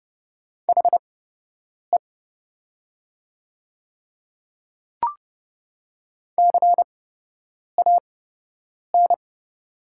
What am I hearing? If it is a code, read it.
Morse code, heard as a keyed tone, about 30 wpm, spaced as for 8 wpm, 700 Hz. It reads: HE CAN
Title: 8x Character Spacing - Top 200 Words - 30wpm